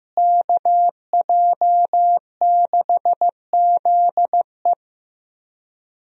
Morse code, heard as a keyed tone, about 15 wpm, 700 Hz. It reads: KJ6ZE
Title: USA Call Signs - 5 Characters - Repeated 3x - 15wpm